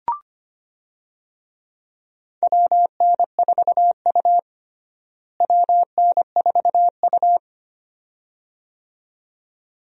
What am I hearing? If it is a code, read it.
Morse code, heard as a keyed tone, about 25 wpm, 700 Hz. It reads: WN4U WN4U
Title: USA Call Signs - 4 Characters - Repeated 2x - 25wpm